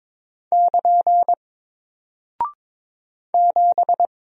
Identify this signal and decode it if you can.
Morse code, heard as a keyed tone, about 22 wpm, 700 Hz. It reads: <KN> 7
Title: Single Character (Letter:Number:Punctuation:Prosign) 22wpm